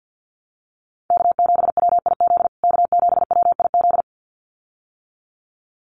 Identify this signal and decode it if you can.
Morse code, heard as a keyed tone, about 50 wpm, 700 Hz. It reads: K7WIZ K7WIZ
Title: USA Call Signs Repeated 2x 50wpm